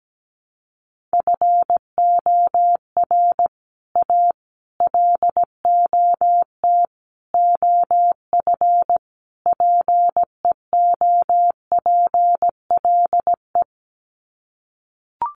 Morse code, heard as a keyed tone, about 17 wpm, 700 Hz. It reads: FOR A LOT OF PEOPLE
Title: Sets of 5 Words Encore 17wpm